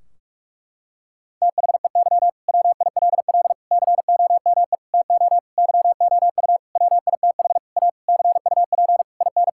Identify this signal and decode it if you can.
Morse code, heard as a keyed tone, about 41 wpm, 700 Hz. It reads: THEY WILL COME TO YOU WITH A CUP IN
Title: Sentences from Top 1600 Words 45wpm